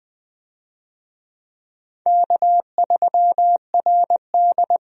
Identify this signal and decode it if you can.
Morse code, heard as a keyed tone, about 20 wpm, 700 Hz. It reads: K3RD
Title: USA Call Signs - 4 Characters - 20wpm